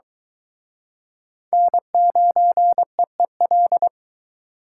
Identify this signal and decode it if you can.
Morse code, heard as a keyed tone, about 23 wpm, 700 Hz. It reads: N9EEL